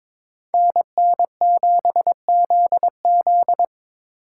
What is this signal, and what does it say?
Morse code, heard as a keyed tone, about 25 wpm, 700 Hz. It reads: NN7ZZ
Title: Speed Racing - USA 2x2 Call Signs 22wpm